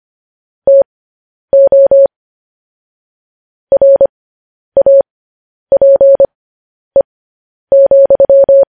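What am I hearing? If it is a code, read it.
Morse code, heard as a keyed tone, about 25 wpm, spaced as for 10 wpm, 550 Hz. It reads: TO RAPE,
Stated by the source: Morse code sending of news headlines, speed 10.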